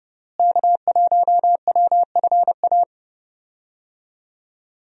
Morse code, heard as a keyed tone, about 32 wpm, 700 Hz. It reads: K1WFA
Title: USA Call Signs - 5 Characters - Repeated 2x - 30wpm